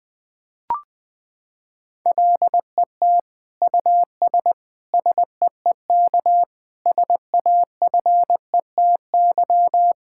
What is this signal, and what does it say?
Morse code, heard as a keyed tone, about 20 wpm, 700 Hz. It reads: LET US SEEK SAFETY
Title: Sentences from Top 1600 Words 20wpm